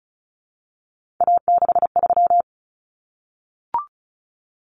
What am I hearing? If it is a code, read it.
Morse code, heard as a keyed tone, about 34 wpm, 700 Hz. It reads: A63
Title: Common Cut Numbers - 3-Digit 35wpm